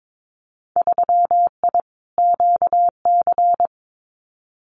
Morse code, heard as a keyed tone, about 22 wpm, 700 Hz. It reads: 3I QC
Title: Winter Field Day Exchanges Encore 22wpm